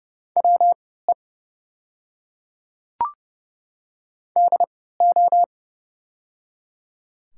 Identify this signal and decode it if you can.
Morse code, heard as a keyed tone, about 30 wpm, spaced as for 17 wpm, 700 Hz. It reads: WE DO
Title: 3x Character Spacing - Top 300 Words - 30wpm